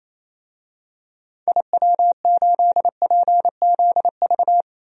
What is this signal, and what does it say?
Morse code, heard as a keyed tone, about 28 wpm, 700 Hz. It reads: IW8PZV
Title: DXCC-100 Call Signs - 6 Characters - Repeated 3x - 28wpm